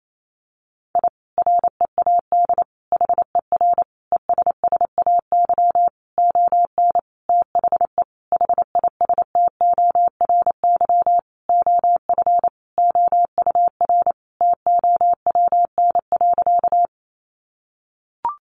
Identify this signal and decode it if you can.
Morse code, heard as a keyed tone, about 28 wpm, 700 Hz. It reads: I READ HER ESSAY ON THE HISTORY OF OUR TOWN.
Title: Sentences from Top 1500 Words 28wpm